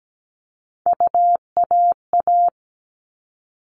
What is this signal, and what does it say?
Morse code, heard as a keyed tone, about 17 wpm, 700 Hz. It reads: UAA
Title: All Cut Numbers - 3-Digit 17wpm